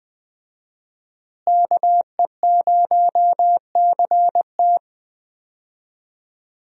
Morse code, heard as a keyed tone, about 21 wpm, 700 Hz. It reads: KE0CT
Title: USA Call Signs - 5 Characters - Repeated 3x - 20wpm